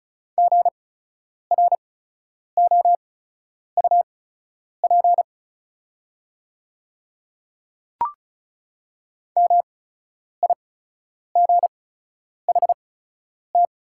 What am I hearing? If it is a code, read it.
Morse code, heard as a keyed tone, about 34 wpm, spaced as for 10 wpm, 690 Hz. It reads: GROUP MIGHT
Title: 8x Character Spacing - Top 500 Words - 35wpm